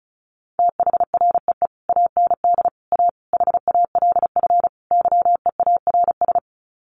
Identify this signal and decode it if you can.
Morse code, heard as a keyed tone, about 34 wpm, 700 Hz. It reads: THREE AND A HALF YEARS
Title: Sets of 5 Words Encore2 35wpm